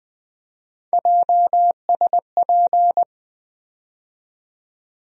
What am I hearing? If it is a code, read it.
Morse code, heard as a keyed tone, about 20 wpm, 700 Hz. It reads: JSP
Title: USA Call Sign Postfixes - 3 Letters 20wpm